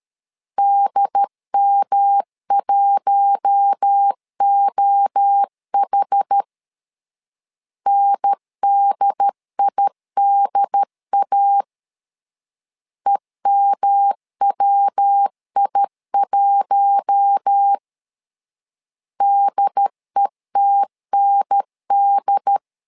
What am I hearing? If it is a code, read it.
Morse code, heard as a keyed tone, about 13 wpm, 800 Hz. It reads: DM1OH NDIDA EMWI1 DETND
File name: LEZIONI / 4 Q1 12wpm 2028